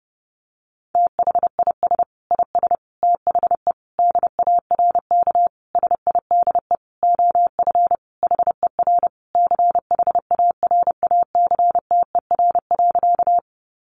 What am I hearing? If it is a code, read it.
Morse code, heard as a keyed tone, about 30 wpm, 700 Hz. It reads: THIS IS THE DARK SIDE OF HER CHARACTER.
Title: Sentences from Top 1200 Words 30wpm